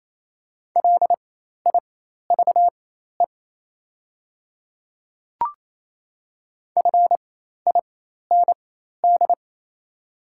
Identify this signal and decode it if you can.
Morse code, heard as a keyed tone, about 27 wpm, spaced as for 13 wpm, 690 Hz. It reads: LIVE FIND
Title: 4x Character Spacing - Top 200 Words - 28wpm